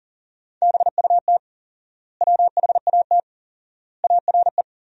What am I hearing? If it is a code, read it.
Morse code, heard as a keyed tone, about 40 wpm, 700 Hz. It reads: BUT WHAT ARE
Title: 4x Word-Spacing - Sentences from Top 100 Words - 40wpm